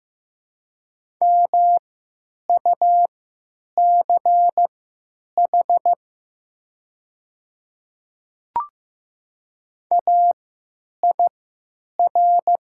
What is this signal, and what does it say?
Morse code, heard as a keyed tone, about 15 wpm, spaced as for 9 wpm, 700 Hz. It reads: MUCH AIR
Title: 3x Character Spacing - Top 200 Words - 15wpm